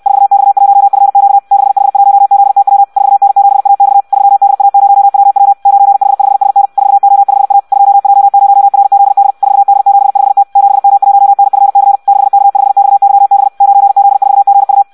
Morse code, heard as a keyed tone, about 64 wpm, 800 Hz. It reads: 4X1FQ 6H1ZTW 4N7AX 3DA0WW 9H5SN 4O5A 2J0RZD 4D75T 7M1NFC 6K5XYB 9Z4GU
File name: Audio practica CW / qrq-tx-66 2